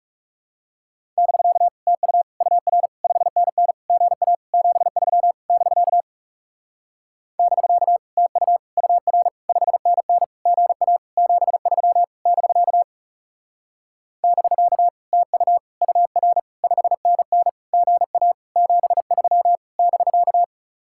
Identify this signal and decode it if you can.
Morse code, heard as a keyed tone, about 40 wpm, 700 Hz. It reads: <BK> TU UR 5NN GA 73 <BK> <BK> TU UR 5NN GA 73 <BK> <BK> TU UR 5NN GA 73 <BK>